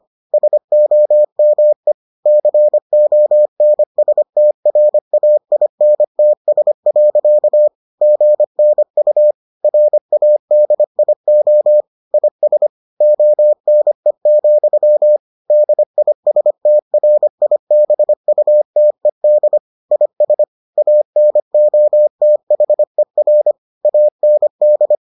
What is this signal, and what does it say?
Morse code, heard as a keyed tone, about 25 wpm, 600 Hz. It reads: SOME CONSTRAINTS. GNU RADIO IS ONE, DISTRIBUTED IS ANOTHER AND